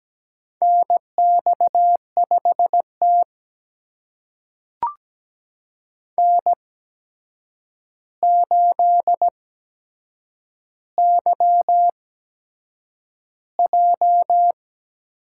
Morse code, heard as a keyed tone, about 17 wpm, spaced as for 7 wpm, 700 Hz. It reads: NX5T N8YJ
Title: USA Call Signs - 4 Characters - 8x Character Spacing - 17wpm